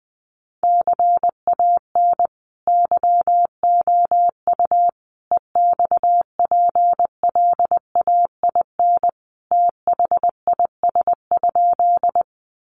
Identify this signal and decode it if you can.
Morse code, heard as a keyed tone, about 20 wpm, 700 Hz. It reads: CAN YOU EXPLAIN THIS?
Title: Sentences from Top 1100 Words 20wpm